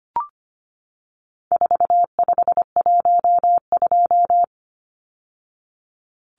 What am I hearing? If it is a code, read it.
Morse code, heard as a keyed tone, about 25 wpm, 700 Hz. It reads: 4512
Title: Numbers - 4-Digit 25wpm